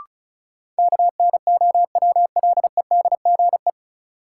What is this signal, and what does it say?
Morse code, heard as a keyed tone, about 36 wpm, 700 Hz. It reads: KNOWLEDGE
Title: Sentences from Top 600 Words - New Word Drill - 35wpm